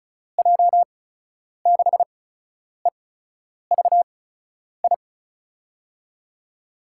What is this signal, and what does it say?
Morse code, heard as a keyed tone, about 36 wpm, spaced as for 10 wpm, 700 Hz. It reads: J6EVI
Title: USA Call Signs - 6 Characters - 8x Character Spacing - 35wpm